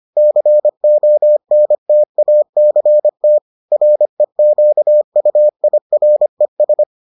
Morse code, heard as a keyed tone, about 25 wpm, 600 Hz. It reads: CONTACT REQUIRES